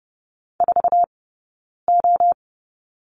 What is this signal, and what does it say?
Morse code, heard as a keyed tone, about 30 wpm, spaced as for 9 wpm, 690 Hz. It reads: 4O
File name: Lesson 19d - Letter w - Two Characters 30wpm